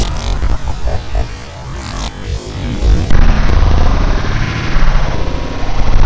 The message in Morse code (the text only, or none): none